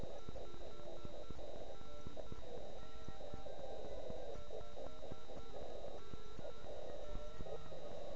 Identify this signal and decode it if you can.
Morse code, heard as a keyed tone, about 9 wpm, 1340 Hz. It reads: 5NN 5NN